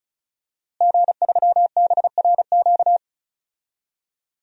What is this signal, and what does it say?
Morse code, heard as a keyed tone, about 36 wpm, 700 Hz. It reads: G3BRQ
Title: DXCC-100 Call Signs - 5 Characters - Repeated 2x - 35wpm